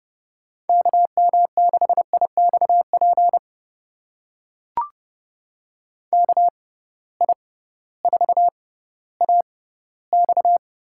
Morse code, heard as a keyed tone, about 31 wpm, spaced as for 16 wpm, 700 Hz. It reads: KM6IXP KI4AX
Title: USA Call Signs - 6 Characters - 6x Character Spacing - 30wpm